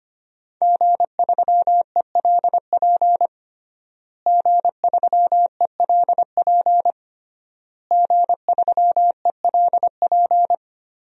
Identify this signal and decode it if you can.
Morse code, heard as a keyed tone, about 25 wpm, 700 Hz. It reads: G3ELP G3ELP G3ELP
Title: DXCC-100 Call Signs - 5 Characters - Repeated 3x - 25wpm